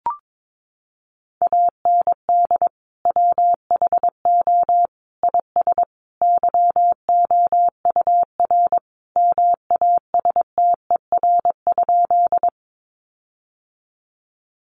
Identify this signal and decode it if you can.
Morse code, heard as a keyed tone, about 22 wpm, 700 Hz. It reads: AND WHO IS YOUR MASTER?
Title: Sentences from Top 1300 Words 22wpm